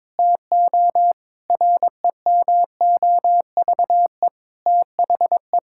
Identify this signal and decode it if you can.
Morse code, heard as a keyed tone, about 22 wpm, 700 Hz. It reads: TO REMOVE THE
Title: Sentences from Top 1000 Words - Review Part 6 - 22wpm